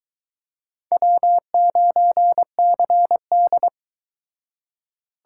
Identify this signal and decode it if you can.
Morse code, heard as a keyed tone, about 25 wpm, 700 Hz. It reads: W9CD